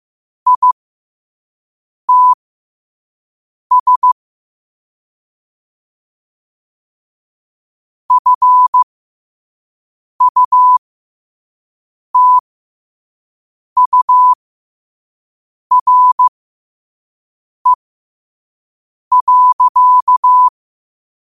Morse code, heard as a keyed tone, about 15 wpm, spaced as for 5 wpm, 1000 Hz. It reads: ITS FUTURE.
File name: qotdfiles / 5wpm